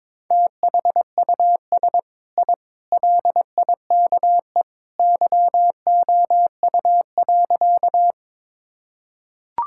Morse code, heard as a keyed tone, about 22 wpm, 700 Hz. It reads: THUS I LIKE YOU.
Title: Sentences from Top 1000 Words - Review Part 5 - 22wpm